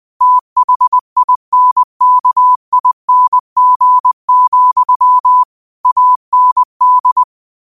Morse code, heard as a keyed tone, about 20 wpm, 1000 Hz. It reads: THINKING, AND
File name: qotdfiles / 20wpm